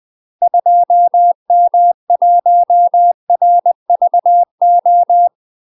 Morse code, heard as a keyed tone, about 20 wpm, 700 Hz. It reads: 2M1RVO
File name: UK MASTER QSOs TRAINING/20WPM